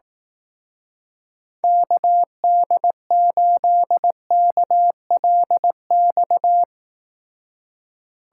Morse code, heard as a keyed tone, about 18 wpm, 700 Hz. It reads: KD8KLX
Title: Speed Racing - USA 2x3 Call Signs 15wpm